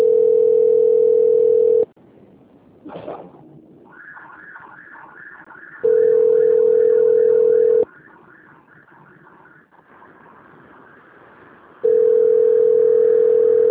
Ringback tone (North America), 440 + 480 Hz.